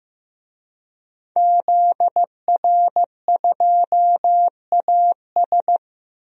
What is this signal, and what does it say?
Morse code, heard as a keyed tone, about 15 wpm, 700 Hz. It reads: ZR2AS
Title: DXCC-100 Call Signs - 5 Characters - Repeated 3x - 15wpm